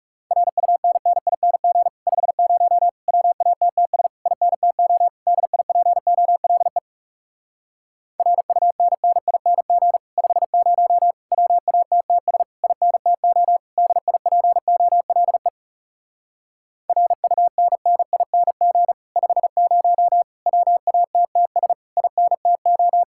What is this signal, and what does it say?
Morse code, heard as a keyed tone, about 37 wpm, 700 Hz. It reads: RUNNING 50 WATTS INTO DIPOLE RUNNING 50 WATTS INTO DIPOLE RUNNING 50 WATTS INTO
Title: Speed-Racing Common QSO Phrases - 30wpm